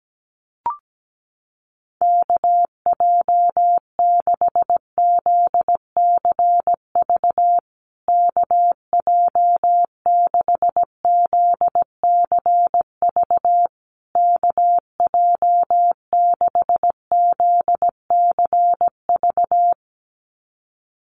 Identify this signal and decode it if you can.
Morse code, heard as a keyed tone, about 17 wpm, 700 Hz. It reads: KJ6ZCV KJ6ZCV KJ6ZCV
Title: USA Call Signs Repeated 3x 17wpm